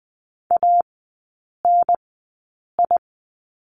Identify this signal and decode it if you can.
Morse code, heard as a keyed tone, about 20 wpm, spaced as for 8 wpm, 700 Hz. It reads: ANI
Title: 053 Lesson 10e - Letter r - Three Characters 20wpm